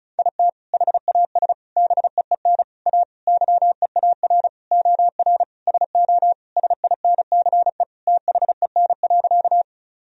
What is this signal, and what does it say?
Morse code, heard as a keyed tone, about 36 wpm, 700 Hz. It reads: IT HAS BEEN A YEAR OR SO SINCE THEN.
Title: Sentences from Top 1000 Words - Review Part 1 - 35wpm